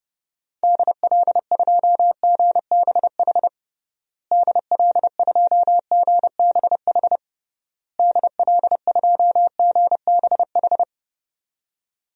Morse code, heard as a keyed tone, about 32 wpm, 700 Hz. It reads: DL2GBH DL2GBH DL2GBH
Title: DXCC-100 Call Signs - 6 Characters - Repeated 3x - 30wpm